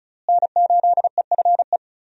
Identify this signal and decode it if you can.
Morse code, heard as a keyed tone, about 36 wpm, 700 Hz. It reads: N8EFE
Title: USA Call Signs - 5 Characters - Repeated 3x - 35wpm